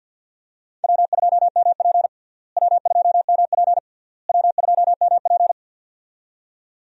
Morse code, heard as a keyed tone, about 48 wpm, 700 Hz. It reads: W2MP W2MP W2MP
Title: USA Call Signs - 4 Characters - Repeated 3x - 50wpm